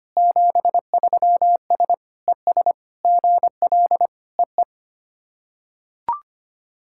Morse code, heard as a keyed tone, about 25 wpm, 690 Hz. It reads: 73S ES GL EE
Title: SOTA QSO Practice 25wpm